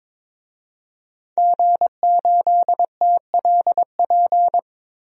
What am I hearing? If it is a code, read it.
Morse code, heard as a keyed tone, about 22 wpm, 700 Hz. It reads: G8TLP
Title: DXCC-100 Call Signs - 5 Characters - 22wpm